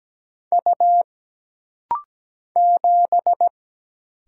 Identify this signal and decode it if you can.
Morse code, heard as a keyed tone, about 17 wpm, 700 Hz. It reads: U 7
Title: Single Letter-Number Alt - Rapid-Fire 17wpm